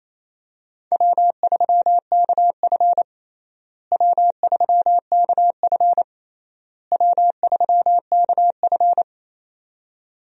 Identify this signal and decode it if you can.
Morse code, heard as a keyed tone, about 28 wpm, 700 Hz. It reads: W3KF W3KF W3KF
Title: USA Call Signs - 4 Characters - Repeated 3x - 28wpm